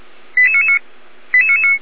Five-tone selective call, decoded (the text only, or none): 88999